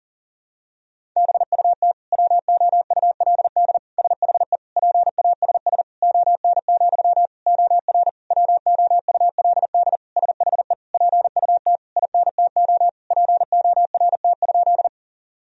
Morse code, heard as a keyed tone, about 40 wpm, 700 Hz. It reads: BUT WOULD SHE PASS ON, OR WOULD SHE PUT INTO PORT?
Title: Sentences from Top 1000 Words - Review Part 1 - 40wpm